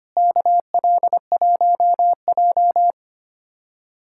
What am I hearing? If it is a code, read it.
Morse code, heard as a keyed tone, about 25 wpm, 700 Hz. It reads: KL1J
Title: USA Callsign DE Callsign - 4 Characters - 25wpm